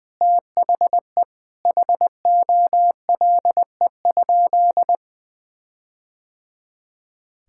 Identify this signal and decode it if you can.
Morse code, heard as a keyed tone, about 20 wpm, 700 Hz. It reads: THE HOLE?
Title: Sentences from Top 1600 Words 20wpm